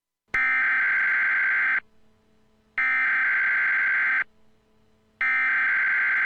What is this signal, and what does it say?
Emergency Alert System (SAME) tones (header bursts), heard as harsh buzzing bursts. It is heard from 0.3 to 6.0 s.
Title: Dead Reign EAS From President Kelly